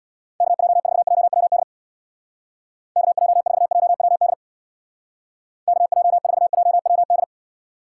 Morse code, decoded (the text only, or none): KJ4JWD KJ4JWD KJ4JWD